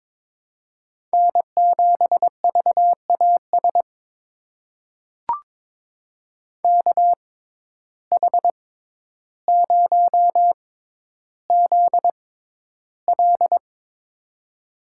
Morse code, heard as a keyed tone, about 21 wpm, spaced as for 11 wpm, 700 Hz. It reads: N7VAS KH0ZL
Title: USA Call Signs - 5 Characters - 6x Character Spacing - 22wpm